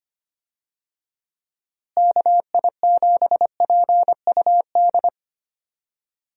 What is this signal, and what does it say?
Morse code, heard as a keyed tone, about 25 wpm, 700 Hz. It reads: KI7PUD
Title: USA Call Signs - 6 Characters - Repeated 2x - 25wpm